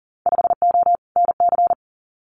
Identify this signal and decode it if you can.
Morse code, heard as a keyed tone, about 42 wpm, 700 Hz. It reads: 5O NC
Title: Winter Field Day Exchanges 40wpm